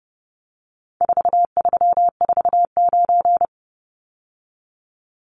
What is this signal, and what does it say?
Morse code, heard as a keyed tone, about 30 wpm, 700 Hz. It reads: 4349
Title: Numbers - 4-Digit 30wpm